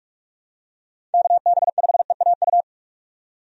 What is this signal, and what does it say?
Morse code, heard as a keyed tone, about 42 wpm, 700 Hz. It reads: KB5EAU
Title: USA Call Signs - 6 Characters - 45wpm